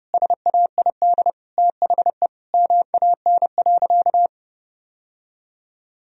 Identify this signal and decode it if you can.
Morse code, heard as a keyed tone, about 30 wpm, 700 Hz. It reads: SAID THE MAN.
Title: Sentences from Top 1000 Words - Review Part 3 - 30wpm